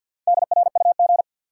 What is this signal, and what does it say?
Morse code, heard as a keyed tone, about 52 wpm, 700 Hz. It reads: DRUG
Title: Sets of 5 Words - New Word Drill - 50wpm